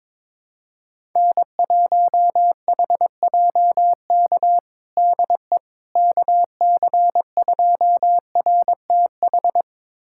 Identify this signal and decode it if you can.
Morse code, heard as a keyed tone, about 22 wpm, 700 Hz. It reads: N1HJK DE KC2RTH
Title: USA Callsign DE Callsign 22wpm